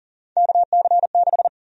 Morse code, heard as a keyed tone, about 40 wpm, 700 Hz. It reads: KC6
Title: USA Call Sign Prefixes - 40wpm